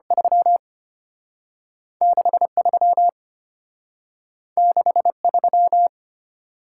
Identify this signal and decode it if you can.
Morse code, heard as a keyed tone, about 34 wpm, 700 Hz. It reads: 3 63 63